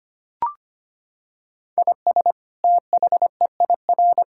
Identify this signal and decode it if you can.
Morse code, heard as a keyed tone, about 25 wpm, 700 Hz. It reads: IS THEIR
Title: Sentences from Top 1000 Words - Review Part 5 - 25wpm